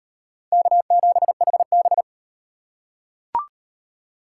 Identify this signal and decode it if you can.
Morse code, heard as a keyed tone, about 37 wpm, 700 Hz. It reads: K7HB